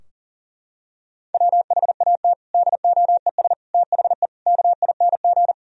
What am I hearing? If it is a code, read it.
Morse code, heard as a keyed tone, about 37 wpm, 700 Hz. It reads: WHAT DOES THE KING
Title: Sentences from Top 1000 Words - Review Part 2 - 40wpm